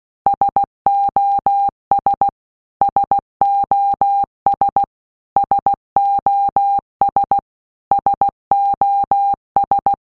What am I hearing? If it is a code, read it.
Morse code, heard as a keyed tone, about 16 wpm, 800 Hz. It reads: SOS SOS SOS SOS
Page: SOS sound in Morse Code (Slow)